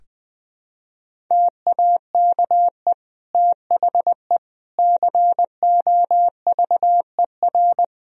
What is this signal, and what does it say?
Morse code, heard as a keyed tone, about 20 wpm, 700 Hz. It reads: TAKE THE COVER
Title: Sentences from Top 1000 Words - Review Part 1 - 20wpm